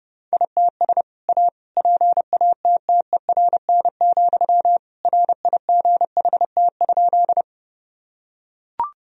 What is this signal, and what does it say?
Morse code, heard as a keyed tone, about 29 wpm, 700 Hz. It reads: ITS A PATTERN, RIGHT?
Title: Sentences from Top 1000 Words - Review Part 2 - 30wpm